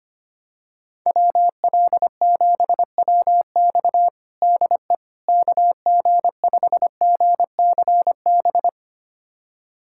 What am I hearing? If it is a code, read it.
Morse code, heard as a keyed tone, about 25 wpm, 700 Hz. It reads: WL7WX DE KG5GCB
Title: USA Callsign DE Callsign 25wpm